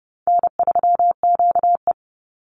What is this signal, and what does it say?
Morse code, heard as a keyed tone, about 30 wpm, 700 Hz. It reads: N3QE
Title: Easy Call Signs Repeated 2x - 30wpm